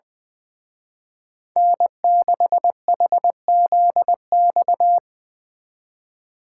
Morse code, heard as a keyed tone, about 22 wpm, 700 Hz. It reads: N6HZX
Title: Speed Racing - USA 1x3 Call Signs 20wpm